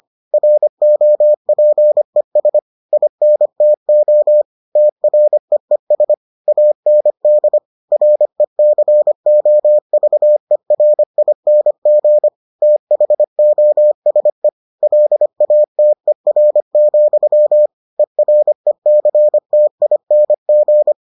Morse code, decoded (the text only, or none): ROPES INTO TREES AND RECOVERING THOSE LATER, ERECTING